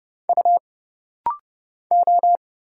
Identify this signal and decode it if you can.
Morse code, heard as a keyed tone, about 29 wpm, 700 Hz. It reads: U O
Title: Single Letters - Rapid-Fire 30wpm